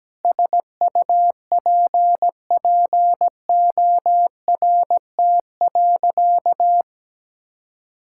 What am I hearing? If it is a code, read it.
Morse code, heard as a keyed tone, about 17 wpm, 700 Hz. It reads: SUPPORT.
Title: Sentences from Top 1000 Words - Review Part 5 - 17wpm